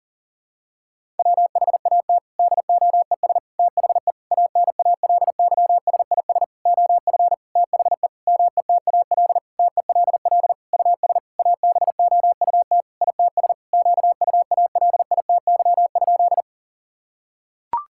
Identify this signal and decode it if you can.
Morse code, heard as a keyed tone, about 40 wpm, 700 Hz. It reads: WHAT DOES THE ANALYSIS OF THE METAL TELL US ABOUT ITS QUALITY?
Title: Sentences from Top 1000 Words - Review Part 6 - 40wpm